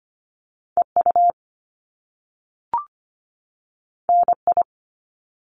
Morse code, heard as a keyed tone, about 25 wpm, 700 Hz. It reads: EU NI
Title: Top Two Letter Combinations in Words 25wpm